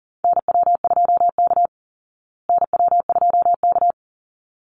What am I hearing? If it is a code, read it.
Morse code, heard as a keyed tone, about 40 wpm, 700 Hz. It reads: NW2K NW2K
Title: Call Signs Repeated 3x - 40wpm